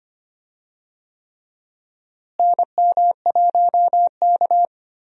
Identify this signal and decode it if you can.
Morse code, heard as a keyed tone, about 25 wpm, 700 Hz. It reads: NM1K
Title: USA Call Signs - 4 Characters - Repeated 3x - 25wpm